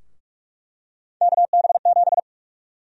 Morse code, heard as a keyed tone, about 46 wpm, 700 Hz. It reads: KB7
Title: USA Call Sign Prefixes - 45wpm